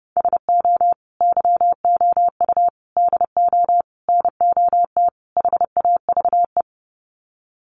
Morse code, heard as a keyed tone, about 30 wpm, 700 Hz. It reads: SO YOU DO NOT HAVE
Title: Sets of 5 Words Encore 30wpm